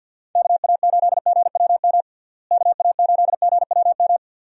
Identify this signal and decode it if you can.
Morse code, heard as a keyed tone, about 48 wpm, 700 Hz. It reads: KA8GWM KA8GWM
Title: USA Call Signs - 6 Characters - Repeated 2x - 50wpm